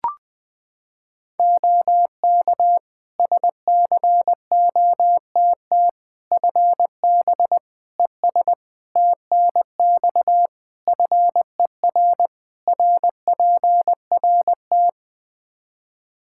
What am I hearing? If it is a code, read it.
Morse code, heard as a keyed tone, about 21 wpm, 700 Hz. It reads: OK SCOTT FB ES TNX FER RPRT
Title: Practice QSO - Exchange 2 - Info - 20wpm